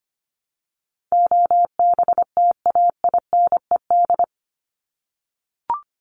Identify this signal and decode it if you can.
Morse code, heard as a keyed tone, about 25 wpm, 700 Hz. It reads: OBTAINED
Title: ED Suffix - Top 100 Words with Base Words 25wpm